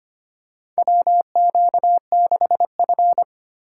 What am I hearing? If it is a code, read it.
Morse code, heard as a keyed tone, about 25 wpm, 700 Hz. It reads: WQ6F
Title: USA Call Signs - 4 Characters - 2x Character Spacing - 25wpm